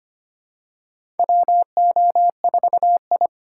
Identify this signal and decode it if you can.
Morse code, heard as a keyed tone, about 25 wpm, 700 Hz. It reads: WO4I
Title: USA Call Signs - 4 Characters - 2x Character Spacing - 25wpm